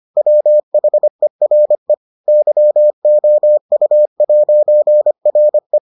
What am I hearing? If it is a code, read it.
Morse code, heard as a keyed tone, about 25 wpm, 600 Hz. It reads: WHERE YOU'RE